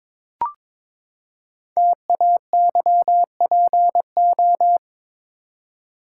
Morse code, heard as a keyed tone, about 22 wpm, 700 Hz. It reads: TAYPO
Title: Code Groups - 5 Letters - 22wpm